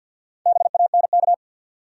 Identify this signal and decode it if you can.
Morse code, heard as a keyed tone, about 50 wpm, 700 Hz. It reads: BANK
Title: Sets of 5 Words - New Word Drill - 50wpm